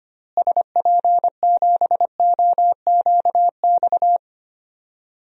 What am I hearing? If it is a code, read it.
Morse code, heard as a keyed tone, about 25 wpm, 700 Hz. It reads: SP7OQX
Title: DXCC-100 Call Signs - 6 Characters - Repeated 2x - 25wpm